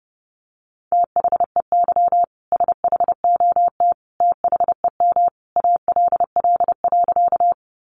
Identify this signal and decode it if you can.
Morse code, heard as a keyed tone, about 30 wpm, 700 Hz. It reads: THEY SHOT THEM ALL.
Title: Sentences from Top 1100 Words 30wpm